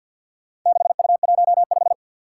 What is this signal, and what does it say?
Morse code, heard as a keyed tone, about 53 wpm, 700 Hz. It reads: 6U15
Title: Four Challenging Letter-Number Combinations 50wpm